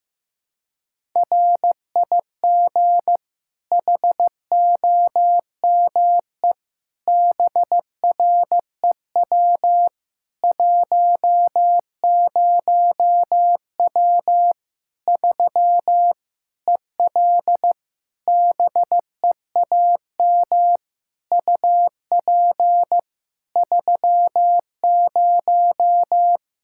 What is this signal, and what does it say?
Morse code, heard as a keyed tone, about 15 wpm, 700 Hz. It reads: RIG HOME BREW 10W 3 EL BEAM UP 30
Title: Practice QSO - Exchange 2 - Info - 15wpm